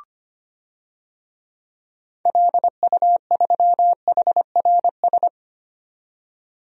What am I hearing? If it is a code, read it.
Morse code, heard as a keyed tone, about 25 wpm, 690 Hz. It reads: LU3HRS
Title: DXCC-100 Call Signs - 6 Characters - 25wpm